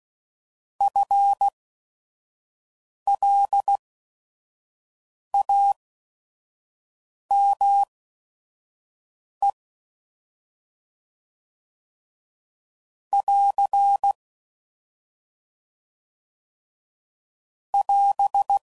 Morse code, heard as a keyed tone, about 16 wpm, spaced as for 5 wpm, 800 Hz. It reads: FLAME <AR> <AS>
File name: Prefixes / By First Letter / Fs